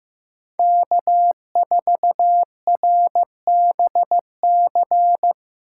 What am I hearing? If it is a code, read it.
Morse code, heard as a keyed tone, about 18 wpm, 700 Hz. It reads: K4RBC
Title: Speed Racing - USA Call Signs 15wpm